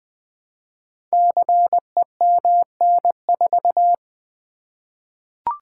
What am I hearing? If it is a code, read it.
Morse code, heard as a keyed tone, about 20 wpm, 700 Hz. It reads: CEMN4
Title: Code Groups - 5 Letter:Numbers - 20wpm